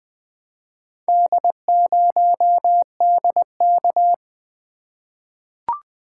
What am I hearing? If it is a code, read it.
Morse code, heard as a keyed tone, about 20 wpm, 700 Hz. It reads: D0DK
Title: Four Letter-Number Combinations - 20wpm